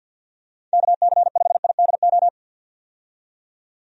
Morse code, heard as a keyed tone, about 52 wpm, 700 Hz. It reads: KC5IDO
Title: USA Call Signs - 6 Characters - 50wpm